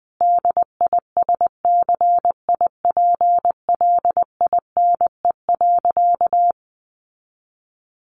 Morse code, heard as a keyed tone, about 20 wpm, 700 Hz. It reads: DISCIPLINE.
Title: Sentences from Top 900 Words 20wpm